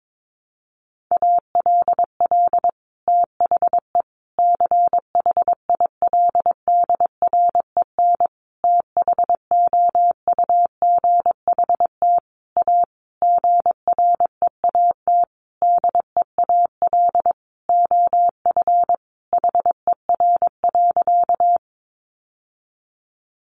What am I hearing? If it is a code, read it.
Morse code, heard as a keyed tone, about 22 wpm, 700 Hz. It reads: ALL THE CHILDREN THOUGHT A GREAT DEAL OF HER.
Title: Sentences from Top 800 Words 22wpm